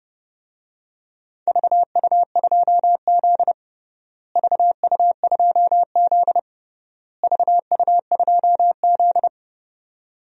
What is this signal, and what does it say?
Morse code, heard as a keyed tone, about 30 wpm, 700 Hz. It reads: VU2Z VU2Z VU2Z
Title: DXCC-100 Call Signs - 4 Characters - Repeated 3x - 30wpm